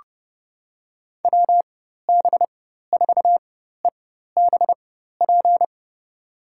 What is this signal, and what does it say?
Morse code, heard as a keyed tone, about 30 wpm, spaced as for 14 wpm, 700 Hz. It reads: WB4EBP
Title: USA Call Signs - 6 Characters - 4x Character Spacing - 30wpm